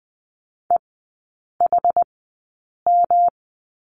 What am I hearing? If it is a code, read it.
Morse code, heard as a keyed tone, about 20 wpm, spaced as for 8 wpm, 700 Hz. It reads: EHM
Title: 166 Lesson 27e - Number 7 - Three Characters 20wpm